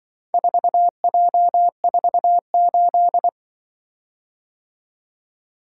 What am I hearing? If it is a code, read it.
Morse code, heard as a keyed tone, about 24 wpm, 700 Hz. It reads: 4J48
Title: Speed-Racing - Four Letter-Number Combinations - 20wpm